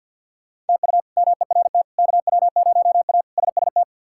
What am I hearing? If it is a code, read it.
Morse code, heard as a keyed tone, about 51 wpm, 700 Hz. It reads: TU MERT KW0A SST
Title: K1USN SST QSO Practice - 50wpm